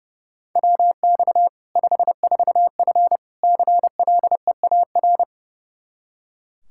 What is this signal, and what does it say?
Morse code, heard as a keyed tone, about 30 wpm, 700 Hz. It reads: WX 54F CLEAR
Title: K1USN SST QSO Practice 30wpm